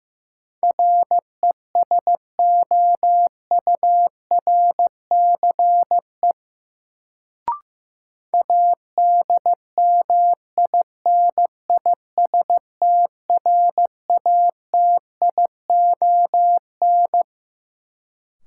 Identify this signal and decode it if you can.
Morse code, heard as a keyed tone, about 15 wpm, 700 Hz. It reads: RESOURCE ADMINISTRATION
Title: Sentences from Top 1300 Words - New Word Drill Alt - 15wpm